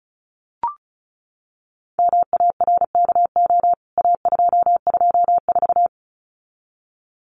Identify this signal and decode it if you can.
Morse code, heard as a keyed tone, about 34 wpm, 700 Hz. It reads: MARKO A224
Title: CWT Exchange 35wpm